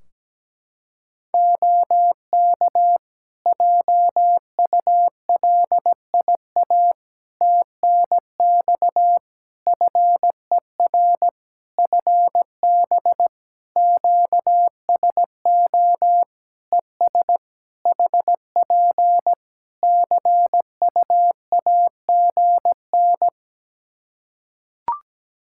Morse code, heard as a keyed tone, about 17 wpm, 700 Hz. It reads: OK JULIA TNX FER FB QSO ES HP CUAGN
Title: Practice QSO - Exchange 4 - Ending - 17wpm